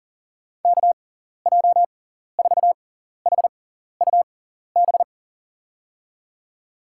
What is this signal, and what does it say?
Morse code, heard as a keyed tone, about 40 wpm, spaced as for 14 wpm, 700 Hz. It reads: KJ4HUB
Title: USA Call Signs - 6 Characters - 6x Character Spacing - 40wpm